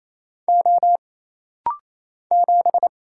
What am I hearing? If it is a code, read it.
Morse code, heard as a keyed tone, about 27 wpm, 700 Hz. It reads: O 7
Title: Single Letter-Number Alt - Rapid-Fire 28wpm